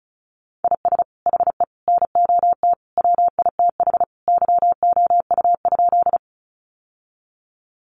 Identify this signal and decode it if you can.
Morse code, heard as a keyed tone, about 34 wpm, 700 Hz. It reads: IS HE NOT WITH YOU?
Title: Sentences from Top 600 Words 35wpm